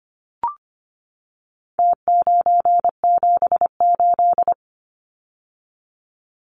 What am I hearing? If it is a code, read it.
Morse code, heard as a keyed tone, about 25 wpm, 700 Hz. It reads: T978
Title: Numbers - Contest Style - 4-Digit 25wpm